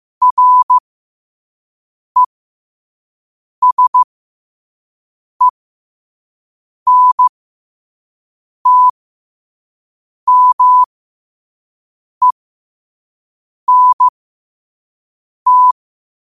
Morse code, heard as a keyed tone, about 15 wpm, spaced as for 5 wpm, 1000 Hz. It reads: RESENTMENT
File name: qotdfiles / 5wpm